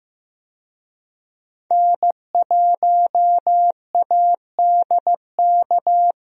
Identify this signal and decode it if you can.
Morse code, heard as a keyed tone, about 15 wpm, 700 Hz. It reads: N1ADK
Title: USA Call Signs - 5 Characters - Repeated 3x - 15wpm